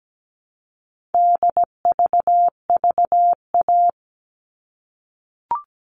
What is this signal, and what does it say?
Morse code, heard as a keyed tone, about 17 wpm, 700 Hz. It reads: DVVA
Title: All Cut Numbers - 4-Digit 17wpm